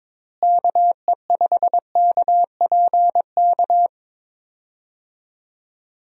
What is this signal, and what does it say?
Morse code, heard as a keyed tone, about 22 wpm, 700 Hz. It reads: KE5KPK
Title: USA Call Signs - 6 Characters - Repeated 2x - 22wpm